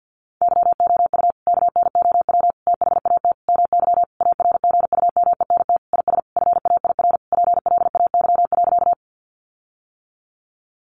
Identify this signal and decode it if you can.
Morse code, heard as a keyed tone, about 50 wpm, 700 Hz. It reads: YOU KNOW THAT MY ARGUMENT IS FAIR PLAY.
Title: Sentences from Top 1400 Words 50wpm